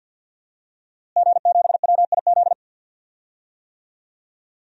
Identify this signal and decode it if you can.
Morse code, heard as a keyed tone, about 52 wpm, 700 Hz. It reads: G7WIZ
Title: DXCC-100 Call Signs - 5 Characters - Repeated 3x - 50wpm